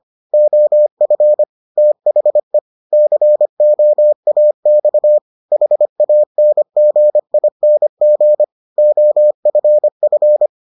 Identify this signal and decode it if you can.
Morse code, heard as a keyed tone, about 25 wpm, 600 Hz. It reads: OF THE COAX HANGING OFF